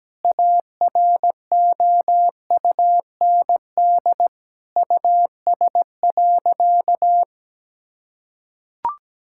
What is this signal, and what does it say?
Morse code, heard as a keyed tone, about 17 wpm, 700 Hz. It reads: AROUND US.
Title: Sentences from Top 1000 Words - Review Part 5 - 17wpm